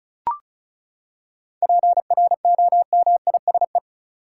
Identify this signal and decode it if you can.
Morse code, heard as a keyed tone, about 36 wpm, 700 Hz. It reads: PROMISE
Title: Seven Letter Words - 35wpm